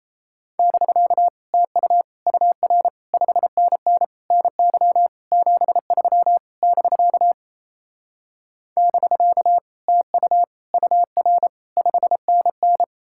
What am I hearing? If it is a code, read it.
Morse code, heard as a keyed tone, about 37 wpm, 700 Hz. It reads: <BK> TU UR 5NN NY 73 <BK> <BK> TU UR 5NN